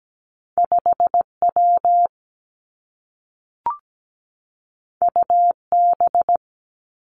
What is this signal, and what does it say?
Morse code, heard as a keyed tone, about 17 wpm, 700 Hz. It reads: 5 W U B
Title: Two Challenging Letter-Number Combinations 17wpm